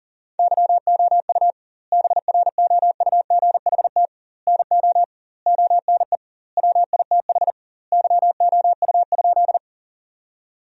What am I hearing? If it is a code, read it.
Morse code, heard as a keyed tone, about 37 wpm, 700 Hz. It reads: YOU BROUGHT NO ONE WITH YOU?
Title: 2x Word-Spacing - Sentences from Top 500 Words - 40wpm